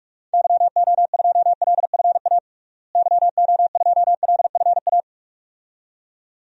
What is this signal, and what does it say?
Morse code, heard as a keyed tone, about 45 wpm, 700 Hz. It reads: YO2LFA YO2LFA
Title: DXCC-100 Call Signs - 6 Characters - Repeated 2x - 45wpm